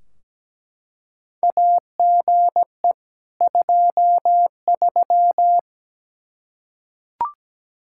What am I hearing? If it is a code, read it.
Morse code, heard as a keyed tone, about 17 wpm, 700 Hz. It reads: AGE 23
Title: K1USN SST QSO Practice 17wpm